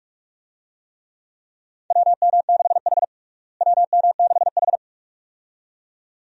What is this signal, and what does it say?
Morse code, heard as a keyed tone, about 46 wpm, 700 Hz. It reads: WM6H WM6H
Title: USA Call Signs - 4 Characters - Repeated 2x - 45wpm